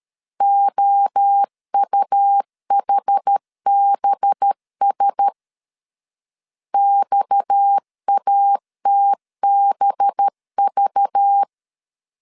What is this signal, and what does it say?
Morse code, heard as a keyed tone, about 13 wpm, 800 Hz. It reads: OUHBS XATBV
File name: CW / DET / 1 6 1500C S B 12wpm 3431